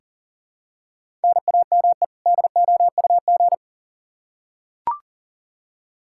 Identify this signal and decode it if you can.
Morse code, heard as a keyed tone, about 40 wpm, 700 Hz. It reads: NAME DOUG
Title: Practice QSO - Exchange 1 - Beginning - 40wpm